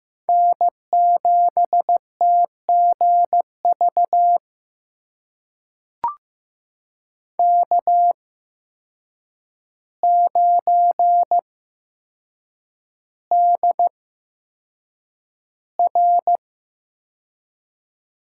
Morse code, heard as a keyed tone, about 15 wpm, spaced as for 7 wpm, 700 Hz. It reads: N7TGV K9DR
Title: USA Call Signs - 5 Characters - 8x Character Spacing - 15wpm